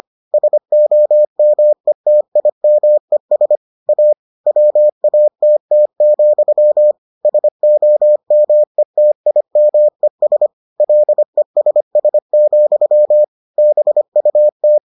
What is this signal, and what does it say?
Morse code, heard as a keyed tone, about 25 wpm, 600 Hz. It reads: SOMETIMES A WATT, SOMETIMES LESS, BUT